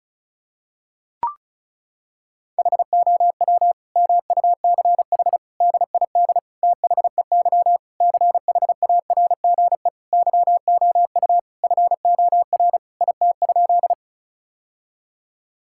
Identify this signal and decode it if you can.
Morse code, heard as a keyed tone, about 36 wpm, 700 Hz. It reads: HOW MUCH DID THEY CHARGE YOU FOR IT?
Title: Sentences from Top 1000 Words - Review Part 6 - 35wpm